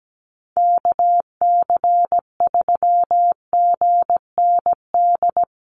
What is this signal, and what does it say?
Morse code, heard as a keyed tone, about 17 wpm, 700 Hz. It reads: KC3GND
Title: USA Call Signs Repeated 2x 17wpm